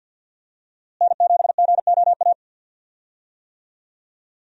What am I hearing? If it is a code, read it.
Morse code, heard as a keyed tone, about 48 wpm, 700 Hz. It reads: N7GOA
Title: USA Call Signs - 5 Characters - Repeated 2x - 50wpm